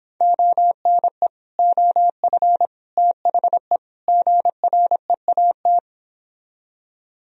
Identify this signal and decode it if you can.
Morse code, heard as a keyed tone, about 26 wpm, 700 Hz. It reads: ONE OF THE GREAT